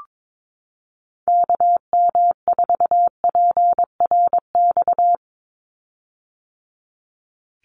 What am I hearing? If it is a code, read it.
Morse code, heard as a keyed tone, about 22 wpm, 700 Hz. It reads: KM4PRX
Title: USA Call Signs 22wpm